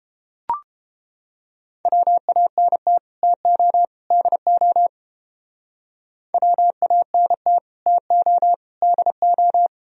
Morse code, heard as a keyed tone, about 29 wpm, 700 Hz. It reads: WANT TO DO WANT TO DO